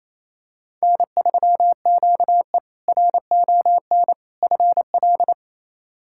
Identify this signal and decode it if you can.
Morse code, heard as a keyed tone, about 28 wpm, 700 Hz. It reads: N3QE RON FL
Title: K1USN SST QSO Practice - 28wpm